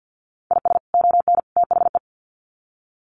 Morse code, heard as a keyed tone, about 50 wpm, 700 Hz. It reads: IS ON THE
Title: Sets of 3 Words 50wpm